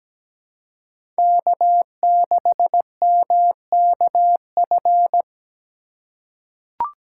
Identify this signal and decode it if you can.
Morse code, heard as a keyed tone, about 17 wpm, 700 Hz. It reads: K6MKF
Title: K1USN SST QSO Practice - 17wpm